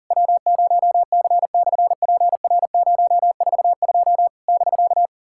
Morse code, heard as a keyed tone, about 40 wpm, 690 Hz. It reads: W0C/PR042 <BK>
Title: SOTA QSO Practice 40wpm